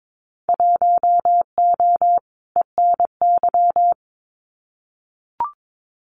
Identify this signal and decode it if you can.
Morse code, heard as a keyed tone, about 22 wpm, 700 Hz. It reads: 1O ENY
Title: Winter Field Day Exchanges 22wpm